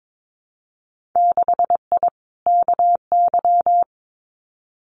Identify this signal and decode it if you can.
Morse code, heard as a keyed tone, about 22 wpm, 700 Hz. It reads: 6I KY
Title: Winter Field Day Exchanges 22wpm